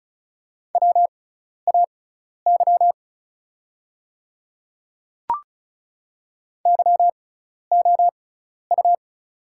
Morse code, heard as a keyed tone, about 36 wpm, spaced as for 12 wpm, 700 Hz. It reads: WAY YOU
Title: 6x Character Spacing - Top 300 Words - 35wpm